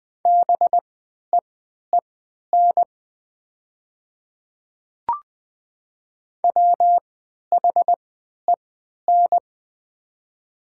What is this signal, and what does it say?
Morse code, heard as a keyed tone, about 20 wpm, spaced as for 11 wpm, 700 Hz. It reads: BEEN WHEN
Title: 3x Character Spacing - Top 300 Words - 20wpm